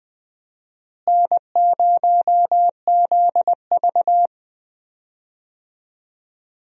Morse code, heard as a keyed tone, about 21 wpm, 700 Hz. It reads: N0ZV
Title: USA Call Signs - 4 Characters - Repeated 2x - 20wpm